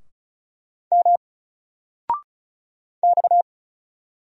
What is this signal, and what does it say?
Morse code, heard as a keyed tone, about 36 wpm, 700 Hz. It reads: M X
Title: Single Character (Letter:Number:Punctuation:Prosign) 35wpm